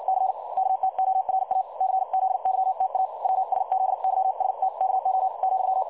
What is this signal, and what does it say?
Morse code, heard as a keyed tone, about 56 wpm, spaced as for 49 wpm, 760 Hz. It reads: C KTORA CZYTA WSZYSTKO ,
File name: dlazaawansowanych / nagraniazpasma / websdr recording 3565.0kHz